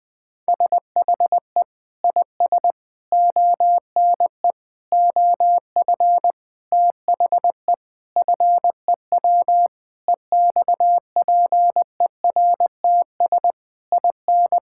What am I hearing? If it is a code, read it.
Morse code, heard as a keyed tone, about 21 wpm, 700 Hz. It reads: SHE IS ONE OF THE FEW EXPERTS IN
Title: Sentences with 5-Word Phrases - 20wpm